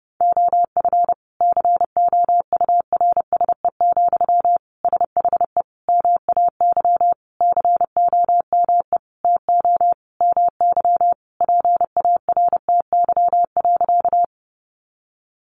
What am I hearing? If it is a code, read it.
Morse code, heard as a keyed tone, about 30 wpm, 700 Hz. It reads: OF COURSE, SHE MAY COME TO MY PARTY.
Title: Sentences from Top 1400 Words 30wpm